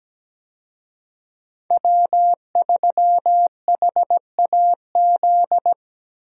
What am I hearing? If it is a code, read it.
Morse code, heard as a keyed tone, about 17 wpm, 700 Hz. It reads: W3HAZ
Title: USA Call Signs - 5 Characters - Repeated 3x - 17wpm